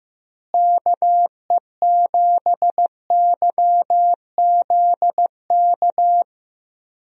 Morse code, heard as a keyed tone, about 15 wpm, 700 Hz. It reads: KE7YZK
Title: USA Call Signs - 6 Characters - Repeated 2x - 15wpm